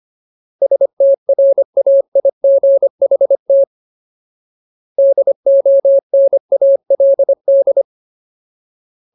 Morse code, heard as a keyed tone, about 25 wpm, 550 Hz. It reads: STRAIGHT DONALD
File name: Ultimate Word List / 25WPM Ultimate